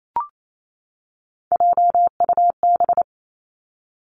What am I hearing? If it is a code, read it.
Morse code, heard as a keyed tone, about 28 wpm, 700 Hz. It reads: JUB
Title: Three Challenging Letter-Number Combinations 28wpm